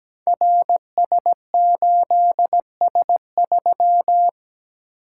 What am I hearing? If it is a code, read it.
Morse code, heard as a keyed tone, about 17 wpm, 700 Hz. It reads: RS8S3
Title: Code Groups - 5 Letter:Numbers - 17wpm